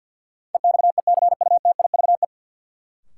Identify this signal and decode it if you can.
Morse code, heard as a keyed tone, about 53 wpm, 700 Hz. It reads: EXECUTIVE
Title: Sets of 5 Words Encore - New Word Drill - 50wpm